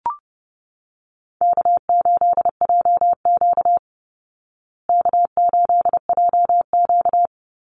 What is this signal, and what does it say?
Morse code, heard as a keyed tone, about 30 wpm, 700 Hz. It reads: K8JQ K8JQ
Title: Call Signs Repeated 3x - 30wpm